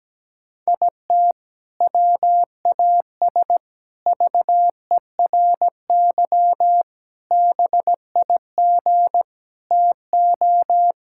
Morse code, heard as a keyed tone, about 17 wpm, 700 Hz. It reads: IT WAS VERY BIG TO
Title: Sentences from Top 1600 Words 17wpm